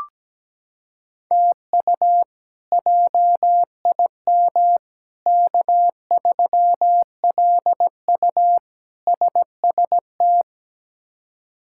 Morse code, heard as a keyed tone, about 17 wpm, 700 Hz. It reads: TU JIM K3LU SST
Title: K1USN SST QSO Practice - 17wpm